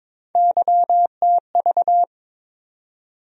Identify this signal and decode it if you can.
Morse code, heard as a keyed tone, about 22 wpm, 700 Hz. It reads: YTV
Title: USA Call Sign Postfixes - 3 Letters 22wpm